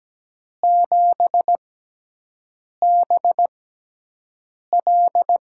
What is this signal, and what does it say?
Morse code, heard as a keyed tone, about 17 wpm, spaced as for 6 wpm, 700 Hz. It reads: 7BL
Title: USA Call Signs - 5 Characters - 6x Character Spacing - 17wpm